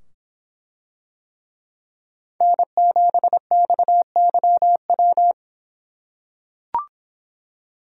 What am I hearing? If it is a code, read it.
Morse code, heard as a keyed tone, about 24 wpm, 700 Hz. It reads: N7XYW